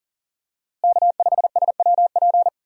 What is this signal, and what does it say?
Morse code, heard as a keyed tone, about 40 wpm, 700 Hz. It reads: K5SWP
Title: USA Call Signs - 5 Characters - Repeated 2x - 40wpm